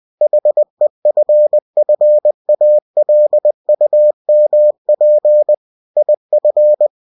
Morse code, heard as a keyed tone, about 20 wpm, 600 Hz. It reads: HEFFALUMP IF
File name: cw-books / winnie-the-pooh / winnie-20wpm-page-0036